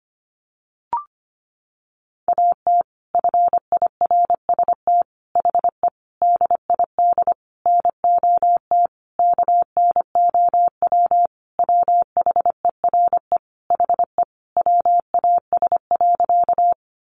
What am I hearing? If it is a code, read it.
Morse code, heard as a keyed tone, about 25 wpm, 700 Hz. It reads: AT FIRST HE DID NOT KNOW WHERE HE WAS.
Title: Sentences from Top 600 Words 25wpm